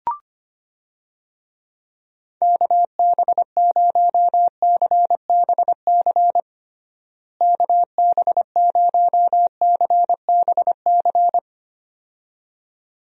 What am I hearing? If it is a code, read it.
Morse code, heard as a keyed tone, about 25 wpm, 700 Hz. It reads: KB0CBC KB0CBC
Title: USA Call Signs - 6 Characters - Repeated 2x - 25wpm